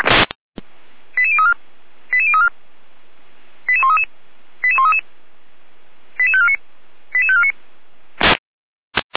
Five-tone selective call (ZVEI), heard as quick runs of five tones, tones of about 70 ms. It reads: 88034 88034 88133 88133 88549 88549